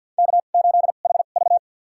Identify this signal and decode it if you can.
Morse code, heard as a keyed tone, about 53 wpm, spaced as for 37 wpm, 700 Hz. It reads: K8HV
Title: USA Call Signs - 4 Characters - 2x Character Spacing - 50wpm